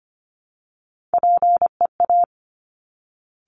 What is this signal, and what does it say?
Morse code, heard as a keyed tone, about 25 wpm, 700 Hz. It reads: PEA
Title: Top Three Letter Combinations in Words 25wpm